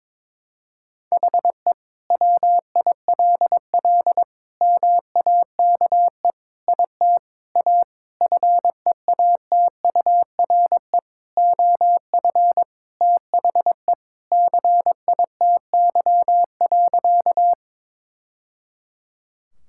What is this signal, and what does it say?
Morse code, heard as a keyed tone, about 23 wpm, 700 Hz. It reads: HE WILL MAKE IT A FEATURE OF THE CITY.
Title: Sentences from Top 1000 Words - Review Part 5 - 22wpm